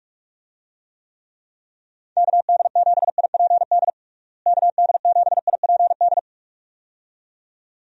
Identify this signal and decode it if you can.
Morse code, heard as a keyed tone, about 42 wpm, 700 Hz. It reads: KD7IPD KD7IPD
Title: USA Call Signs - 6 Characters - Repeated 2x - 45wpm